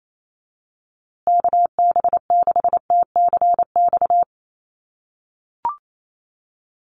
Morse code, heard as a keyed tone, about 28 wpm, 700 Hz. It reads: KB6TCX
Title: USA Call Signs Repeated 2x 28wpm